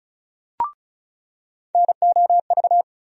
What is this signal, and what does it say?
Morse code, heard as a keyed tone, about 36 wpm, 700 Hz. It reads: NOV
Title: USA Call Sign Postfixes - 3 Letters 35wpm